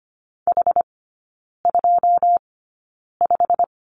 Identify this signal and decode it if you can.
Morse code, heard as a keyed tone, about 25 wpm, spaced as for 9 wpm, 700 Hz. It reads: H25
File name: course / 25:10wpm / 093 Lesson 16e - Letter c - Three Characters 25wpm